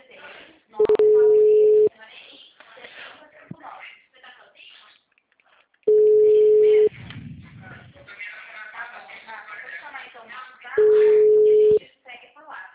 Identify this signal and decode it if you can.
Ringback tone (Europe, 425 Hz), with speech.
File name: out- -2014- - - .34134